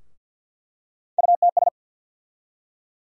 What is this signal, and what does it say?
Morse code, heard as a keyed tone, about 47 wpm, 700 Hz. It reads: UTS
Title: USA Call Sign Postfixes - 3 Letters 50wpm